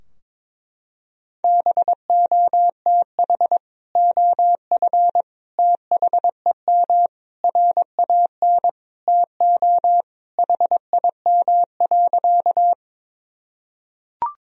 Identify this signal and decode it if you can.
Morse code, heard as a keyed tone, about 22 wpm, 700 Hz. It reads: BOTH OF THEM RAN TO HIM.
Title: Sentences from Top 1000 Words - Review Part 3 - 22wpm